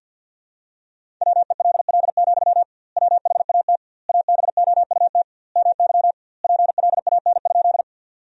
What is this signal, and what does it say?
Morse code, heard as a keyed tone, about 52 wpm, 700 Hz. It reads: WELL, WHAT ABOUT MY PLAN?
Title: 2x Word-Spacing - Sentences from Top 500 Words - 50wpm